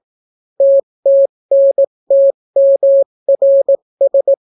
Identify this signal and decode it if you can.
Morse code, heard as a keyed tone, about 18 wpm, 550 Hz. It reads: TTNTMRS